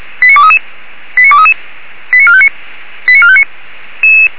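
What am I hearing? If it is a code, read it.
Five-tone selective call (ZVEI), heard as quick runs of five tones, tones of about 80 ms. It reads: 89233 89233 89459 89459
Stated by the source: ZVEI five-tone call received on an SDR.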